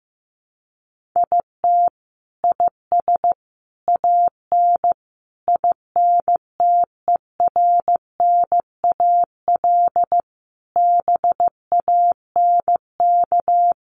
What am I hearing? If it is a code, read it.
Morse code, heard as a keyed tone, about 15 wpm, 700 Hz. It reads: IT IS AN INTERNAL BANK
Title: Sentences from Top 1400 Words 15wpm